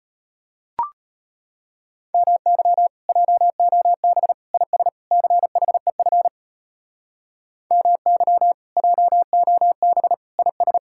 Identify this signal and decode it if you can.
Morse code, heard as a keyed tone, about 34 wpm, 700 Hz. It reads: MY JOB IS CHEF MY JOB IS